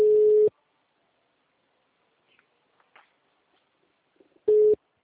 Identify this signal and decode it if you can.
Ringback tone (Europe, 425 Hz).
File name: out- -2011- - - .25261